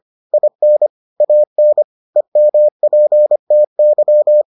Morse code, heard as a keyed tone, about 25 wpm, 600 Hz. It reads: IN AN EMPTY